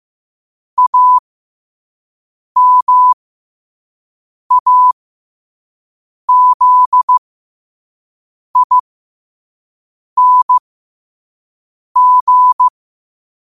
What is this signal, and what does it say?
Morse code, heard as a keyed tone, about 15 wpm, spaced as for 5 wpm, 1000 Hz. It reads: AMAZING
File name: qotdfiles / 5wpm